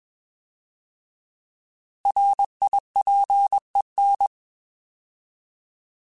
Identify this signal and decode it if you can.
Morse code, heard as a keyed tone, about 21 wpm, 800 Hz. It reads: RIPEN